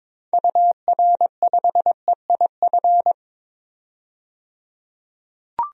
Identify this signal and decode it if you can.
Morse code, heard as a keyed tone, about 22 wpm, 700 Hz. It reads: UR5EIF
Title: DXCC-100 Call Signs - 6 Characters - 22wpm